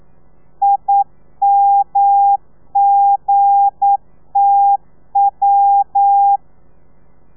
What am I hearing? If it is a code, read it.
Morse code, heard as a keyed tone, about 9 wpm, 800 Hz. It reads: IMGTW
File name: telegrafia / i2tfj / Lez8 40